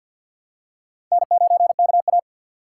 Morse code, heard as a keyed tone, about 53 wpm, 700 Hz. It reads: N9KA
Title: USA Call Signs - 2x Character Spacing - 50wpm